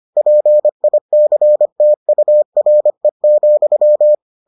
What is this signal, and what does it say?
Morse code, heard as a keyed tone, about 25 wpm, 600 Hz. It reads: PICTURE,